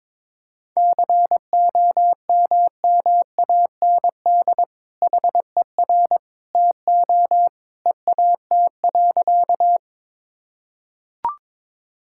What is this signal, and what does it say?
Morse code, heard as a keyed tone, about 22 wpm, 700 Hz. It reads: COMMAND HER TO EAT.
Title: Sentences from Top 1600 Words 22wpm